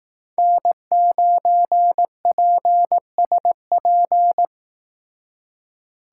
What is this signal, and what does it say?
Morse code, heard as a keyed tone, about 18 wpm, 700 Hz. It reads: N9PSP
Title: Speed Racing - USA 1x3 Call Signs 15wpm